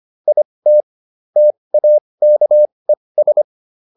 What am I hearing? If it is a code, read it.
Morse code, heard as a keyed tone, about 25 wpm, spaced as for 20 wpm, 600 Hz. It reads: IT TAKES